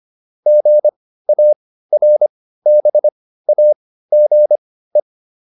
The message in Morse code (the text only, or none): GARBAGE